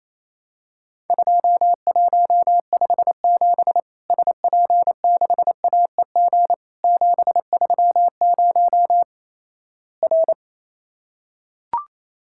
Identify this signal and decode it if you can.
Morse code, heard as a keyed tone, about 28 wpm, 700 Hz. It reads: 2157 SP6AEG 730 R
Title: Worked All Europe DX Contest - QTC Exchange - 28wpm